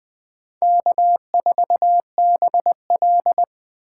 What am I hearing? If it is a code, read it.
Morse code, heard as a keyed tone, about 23 wpm, 700 Hz. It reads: K4BL
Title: Speed Racing - USA 1x2 Call Signs 20wpm